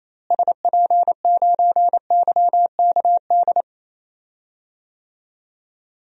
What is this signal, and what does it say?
Morse code, heard as a keyed tone, about 28 wpm, 700 Hz. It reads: SP9YKD
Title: DXCC-100 Call Signs - 6 Characters - Repeated 2x - 28wpm